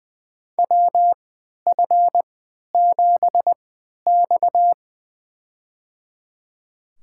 Morse code, heard as a keyed tone, about 20 wpm, spaced as for 11 wpm, 700 Hz. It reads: WF7X
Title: USA Call Signs - 4 Characters - 3x Character Spacing - 20wpm